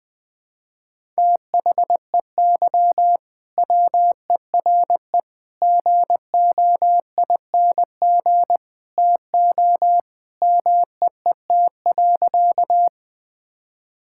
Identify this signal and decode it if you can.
Morse code, heard as a keyed tone, about 20 wpm, 700 Hz. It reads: THEY WERE GOING TO MEET.
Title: Sentences with 3-Word Phrases - 20wpm